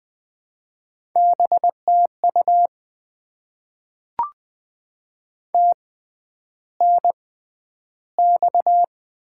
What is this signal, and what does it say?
Morse code, heard as a keyed tone, about 20 wpm, 700 Hz. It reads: B T U T N X
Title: Top 20 QSO Elements - 6x Character Spacing - 20wpm